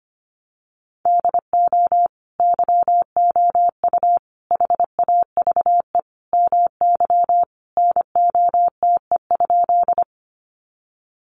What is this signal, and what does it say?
Morse code, heard as a keyed tone, about 25 wpm, 700 Hz. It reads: DO YOU HAVE MY NOTE?
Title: Sentences from Top 500 Words 25wpm